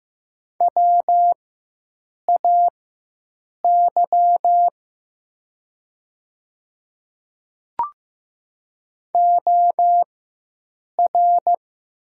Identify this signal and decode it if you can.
Morse code, heard as a keyed tone, about 15 wpm, spaced as for 7 wpm, 700 Hz. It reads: WAY OR
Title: 4x Character Spacing - Top 100 Words - 15wpm